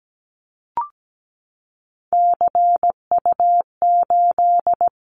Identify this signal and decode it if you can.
Morse code, heard as a keyed tone, about 17 wpm, 700 Hz. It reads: CU8
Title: Callsign Prefixes for Top 50 DXCC - 17wpm